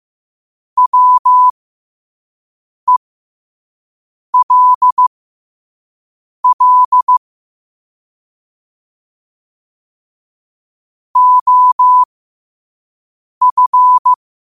Morse code, heard as a keyed tone, about 15 wpm, spaced as for 5 wpm, 1000 Hz. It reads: WELL OF
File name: qotdfiles / 5wpm